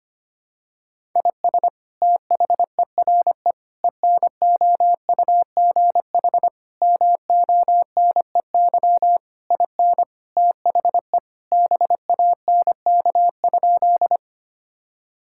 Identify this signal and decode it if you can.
Morse code, heard as a keyed tone, about 25 wpm, 700 Hz. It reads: IS THERE ENOUGH MONEY IN THE BANK?
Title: Sentences from Top 1000 Words - Review Part 6 - 25wpm